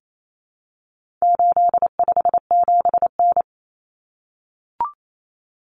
Morse code, heard as a keyed tone, about 28 wpm, 700 Hz. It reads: 857N
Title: Common Cut Numbers - 4-Digit 28wpm